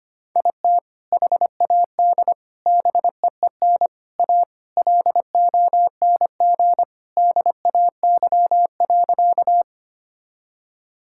Morse code, heard as a keyed tone, about 25 wpm, 700 Hz. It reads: IT HAD BEEN A LONG DAY.
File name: practice / 25wpm / Sentences with 3-Word Phrases - 25wpm